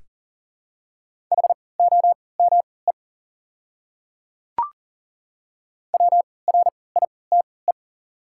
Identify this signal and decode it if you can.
Morse code, heard as a keyed tone, about 39 wpm, spaced as for 22 wpm, 700 Hz. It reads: HOME WRITE
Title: 3x Character Spacing - Top 200 Words - 40wpm